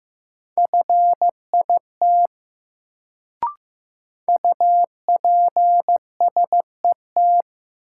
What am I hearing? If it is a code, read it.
Morse code, heard as a keyed tone, about 15 wpm, 700 Hz. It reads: FIT UPSET
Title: Sentences from Top 800 Words - New Word Drill - 15wpm